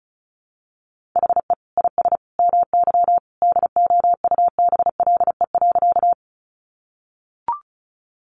Morse code, heard as a keyed tone, about 34 wpm, 700 Hz. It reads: HE IS MY DOUBLE.
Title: Sentences from Top 1400 Words 35wpm